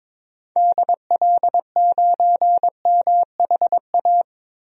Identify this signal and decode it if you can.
Morse code, heard as a keyed tone, about 22 wpm, 700 Hz. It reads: DL9MHA
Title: DXCC-100 Call Signs - 6 Characters - 22wpm